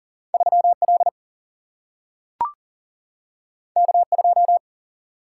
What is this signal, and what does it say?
Morse code, heard as a keyed tone, about 40 wpm, 700 Hz. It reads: 3 L K 2
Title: Single-Letter Single-Number 40wpm